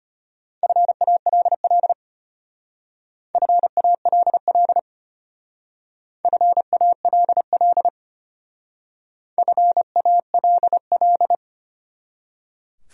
Morse code, heard as a keyed tone, about 32 wpm, 700 Hz. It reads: FALL FALL FALL FALL